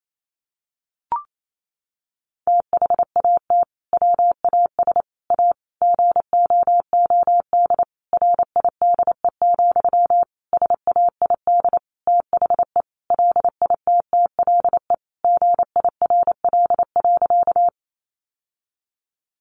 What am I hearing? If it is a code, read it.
Morse code, heard as a keyed tone, about 28 wpm, 700 Hz. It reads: THAT WAS A GOOD RIDE, SAID THE LITTLE GIRL.
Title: Sentences from Top 1200 Words 28wpm